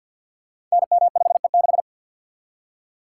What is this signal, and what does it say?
Morse code, heard as a keyed tone, about 53 wpm, 700 Hz. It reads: NM5E6
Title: Code Groups - 5 Letter:Numbers - 50wpm